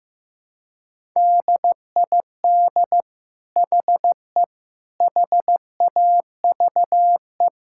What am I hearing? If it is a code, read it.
Morse code, heard as a keyed tone, about 15 wpm, 700 Hz. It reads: DID HE HAVE
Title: Sentences from Top 1000 Words - Review Part 5 - 15wpm